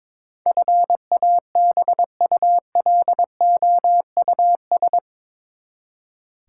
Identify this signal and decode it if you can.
Morse code, heard as a keyed tone, about 22 wpm, 700 Hz. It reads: FABULOUS
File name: practice / 22wpm / Eight Letter Words - 22wpm